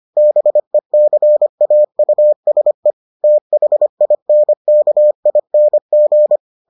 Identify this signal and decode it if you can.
Morse code, heard as a keyed tone, about 25 wpm, 600 Hz. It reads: BECAUSE THINKING